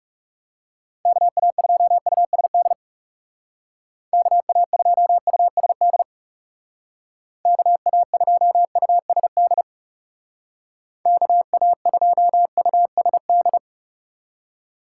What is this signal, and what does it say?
Morse code, heard as a keyed tone, about 36 wpm, 700 Hz. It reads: KA2USD KA2USD KA2USD KA2USD
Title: Speed Racing - USA 2x3 Call Signs 30wpm